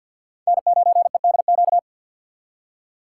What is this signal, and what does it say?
Morse code, heard as a keyed tone, about 62 wpm, 700 Hz. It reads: N9EDQ
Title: Speed Racing - USA 1x3 Call Signs 50wpm